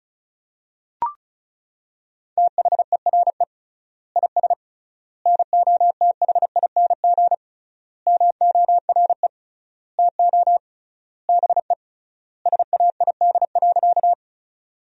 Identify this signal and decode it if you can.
Morse code, heard as a keyed tone, about 34 wpm, 700 Hz. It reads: THERE IS NOTHING MORE TO BE SAID.
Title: 3x Word-Spacing - Sentences from Top 500 Words - 35wpm